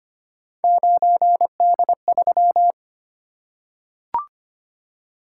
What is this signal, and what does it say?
Morse code, heard as a keyed tone, about 26 wpm, 700 Hz. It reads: 9D3
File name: Three Letter-Number Combinations - 25wpm